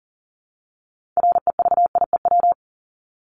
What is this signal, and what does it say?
Morse code, heard as a keyed tone, about 40 wpm, 700 Hz. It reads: REVIEW
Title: Top 751-1000 Words 40wpm